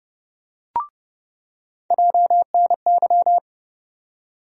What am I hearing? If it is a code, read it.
Morse code, heard as a keyed tone, about 30 wpm, 700 Hz. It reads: JNY
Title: USA Call Sign Postfixes - 3 Letters 30wpm